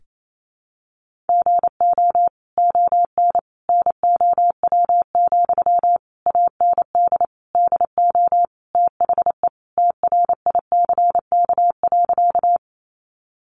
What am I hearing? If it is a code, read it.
Morse code, heard as a keyed tone, about 28 wpm, 700 Hz. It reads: GO ON NOW, AND DO THE TRICK.
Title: Sentences from Top 1500 Words 28wpm